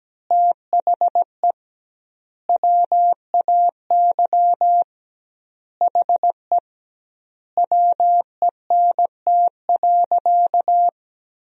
Morse code, heard as a keyed tone, about 17 wpm, 700 Hz. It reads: THE WAY HE WENT.
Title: 2x Word-Spacing - Sentences from Top 200 Words - 17wpm